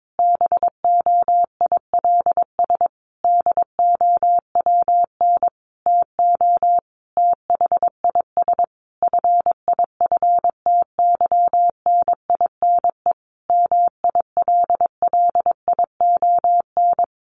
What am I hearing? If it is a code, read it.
Morse code, heard as a keyed tone, about 22 wpm, 700 Hz. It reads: BOILS DOWN TO THIS FIFTYNINE MILLION
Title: A Sound Of Thunder 22wpm